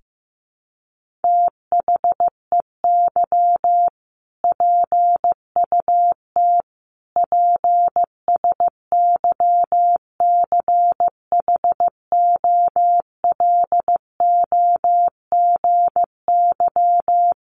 Morse code, heard as a keyed tone, about 15 wpm, 700 Hz. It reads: THEY PUT PSYCHOLOGY
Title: Sentences from Top 1400 Words 15wpm